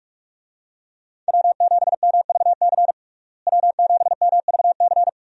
Morse code, heard as a keyed tone, about 46 wpm, 700 Hz. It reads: W7MVC W7MVC
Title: USA Call Signs - 5 Characters - Repeated 2x - 45wpm